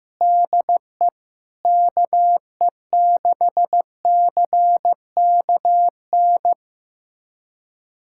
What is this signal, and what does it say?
Morse code, heard as a keyed tone, about 16 wpm, 700 Hz. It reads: DE KE6CKN
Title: USA Callsign DE Callsign 15wpm